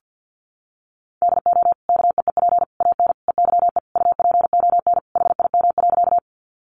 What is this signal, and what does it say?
Morse code, heard as a keyed tone, about 48 wpm, 700 Hz. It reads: DO KEEP AN EYE UPON HIM.
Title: Sentences from Top 1100 Words 50wpm